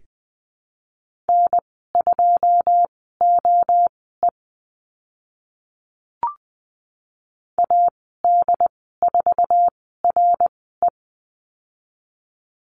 Morse code, heard as a keyed tone, about 20 wpm, spaced as for 14 wpm, 700 Hz. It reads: N2OE AD4RE
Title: 082 Lesson 14g - Number 2 - Call Signs 20wpm